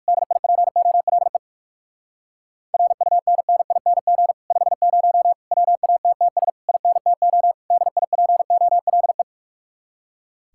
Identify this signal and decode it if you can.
Morse code, heard as a keyed tone, about 60 wpm, 700 Hz. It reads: DIPOLE RUNNING 50 WATTS INTO DIPOLE